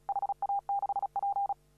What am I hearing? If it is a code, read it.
Morse code, heard as a keyed tone, about 35 wpm, 800 Hz. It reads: HA6P
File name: CW / RHOB CW 2017 HA6P 0001